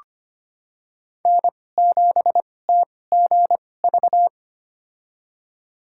Morse code, heard as a keyed tone, about 25 wpm, spaced as for 18 wpm, 700 Hz. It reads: N7TGV
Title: USA Call Signs - 2x Character Spacing - 25wpm